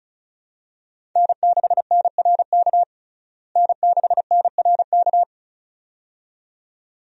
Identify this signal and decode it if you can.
Morse code, heard as a keyed tone, about 34 wpm, 700 Hz. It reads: N6NRK N6NRK
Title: USA Call Signs - 5 Characters - Repeated 2x - 35wpm